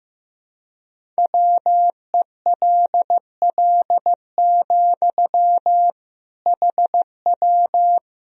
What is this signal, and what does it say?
Morse code, heard as a keyed tone, about 15 wpm, 700 Hz. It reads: WELL, HW
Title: Sentences from Top 200 Words and Abbreviations 15wpm